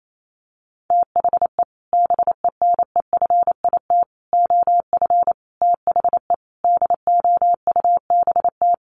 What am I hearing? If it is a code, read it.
Morse code, heard as a keyed tone, about 28 wpm, 700 Hz. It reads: THE BENEFIT OF THE DOUBT
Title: Sets of 5 Words Encore 28wpm